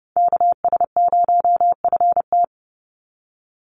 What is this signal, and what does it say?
Morse code, heard as a keyed tone, about 30 wpm, 700 Hz. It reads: KS0FT
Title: USA Call Signs 30wpm